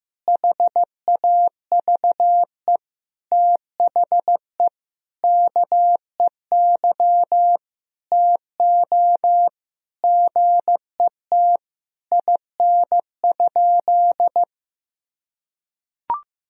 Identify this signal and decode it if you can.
Morse code, heard as a keyed tone, about 15 wpm, 700 Hz. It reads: HAVE THE KEY TO GET IN?
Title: Sentences from Top 1000 Words - Review Part 4 - 15wpm